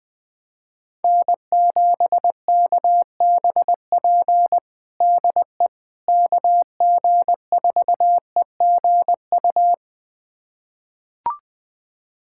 Morse code, heard as a keyed tone, about 20 wpm, 700 Hz. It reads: N7KBP DE KG4EGU
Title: USA Callsign DE Callsign 20wpm